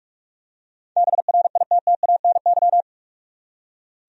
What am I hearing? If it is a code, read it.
Morse code, heard as a keyed tone, about 46 wpm, 700 Hz. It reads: BRITTANY
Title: USA Common Names - 45wpm